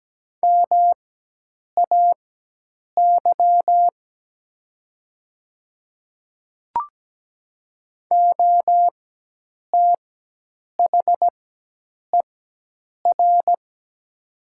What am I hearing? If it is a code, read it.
Morse code, heard as a keyed tone, about 17 wpm, spaced as for 8 wpm, 700 Hz. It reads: MAY OTHER
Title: 4x Character Spacing - Top 200 Words - 17wpm